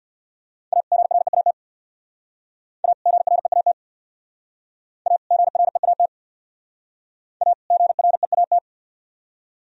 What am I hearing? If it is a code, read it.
Morse code, heard as a keyed tone, about 63 wpm, 700 Hz. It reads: A GREAT A GREAT A GREAT A GREAT